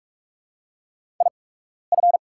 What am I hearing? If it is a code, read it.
Morse code, heard as a keyed tone, about 44 wpm, spaced as for 12 wpm, 700 Hz. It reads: IF
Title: 8x Character Spacing - Top 200 Words - 45wpm